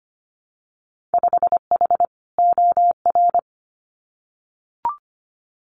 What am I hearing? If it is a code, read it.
Morse code, heard as a keyed tone, about 25 wpm, 690 Hz. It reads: 5H OR
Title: Winter Field Day Exchanges Encore 25wpm